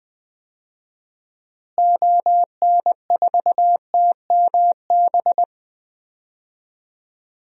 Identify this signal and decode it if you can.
Morse code, heard as a keyed tone, about 20 wpm, 700 Hz. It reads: ON4TMB
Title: DXCC-100 Call Signs - 6 Characters - Repeated 2x - 20wpm